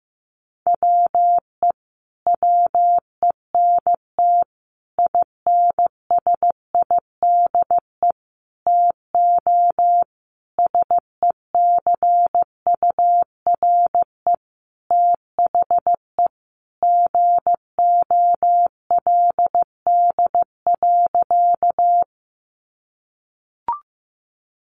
Morse code, heard as a keyed tone, about 15 wpm, 700 Hz. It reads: WE WENT INSIDE TO SECURE THE GOLD.
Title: Sentences from Top 1200 Words 15wpm